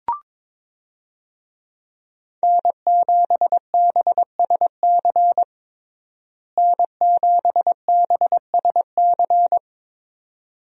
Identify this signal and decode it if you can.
Morse code, heard as a keyed tone, about 22 wpm, 700 Hz. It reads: N7BSC N7BSC
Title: USA Call Signs - 5 Characters - Repeated 2x - 22wpm